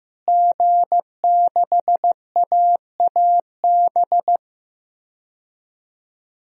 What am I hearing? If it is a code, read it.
Morse code, heard as a keyed tone, about 15 wpm, 700 Hz. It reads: G6AAB
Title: DXCC-100 Call Signs - 5 Characters - Repeated 2x - 15wpm